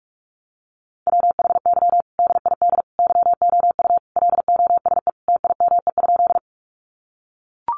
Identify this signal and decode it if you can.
Morse code, heard as a keyed tone, about 44 wpm, 700 Hz. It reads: WHY DID YOU LOSE TIME?
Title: Sentences from Top 1100 Words 45wpm